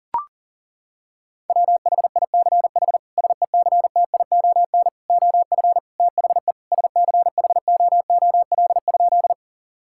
Morse code, heard as a keyed tone, about 40 wpm, 700 Hz. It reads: WHICH SECTION OF THE SCHOOL?
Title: Sentences from Top 1000 Words - Review Part 5 - 40wpm